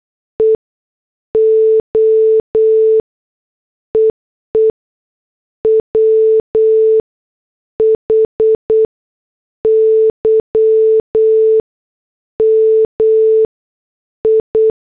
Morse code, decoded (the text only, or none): E O EE WHY MI